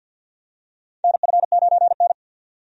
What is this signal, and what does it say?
Morse code, heard as a keyed tone, about 50 wpm, 700 Hz. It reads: NF9N
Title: USA Call Signs - 4 Characters - 50wpm